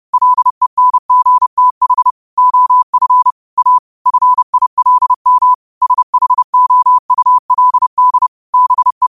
Morse code, heard as a keyed tone, about 30 wpm, 1000 Hz. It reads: LENGTH OF A FILM SHOULD BE
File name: qotdfiles / 30wpm